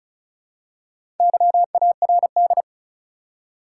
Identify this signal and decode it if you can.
Morse code, heard as a keyed tone, about 34 wpm, 700 Hz. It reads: YARD
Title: Four Letter Words - 35wpm